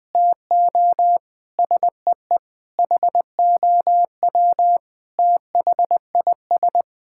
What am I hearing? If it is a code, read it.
Morse code, heard as a keyed tone, about 21 wpm, 700 Hz. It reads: TO SEE HOW THIS
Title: Sentences with 7-Word Phrases - Spoken Beginning - 20wpm